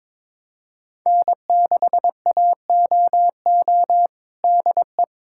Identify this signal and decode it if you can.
Morse code, heard as a keyed tone, about 22 wpm, 700 Hz. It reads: N6AOO DE
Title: USA Callsign DE Callsign - 5 Characters - 22wpm